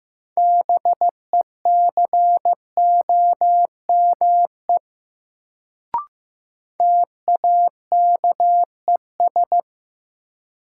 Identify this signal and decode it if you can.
Morse code, heard as a keyed tone, about 15 wpm, 700 Hz. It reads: BECOME TAKES
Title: Sets of 5 Words Encore - New Word Drill - 15wpm